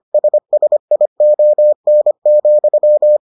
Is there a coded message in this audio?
Yes: Morse code from 0.1 to 3.2 s, heard as a keyed tone, about 25 wpm, 600 Hz. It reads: SSION,